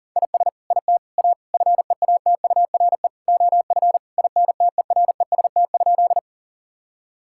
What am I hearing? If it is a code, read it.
Morse code, heard as a keyed tone, about 40 wpm, 700 Hz. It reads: IS IT A FEATURE OF INTEREST?
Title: Sentences from Top 1000 Words - Review Part 5 - 40wpm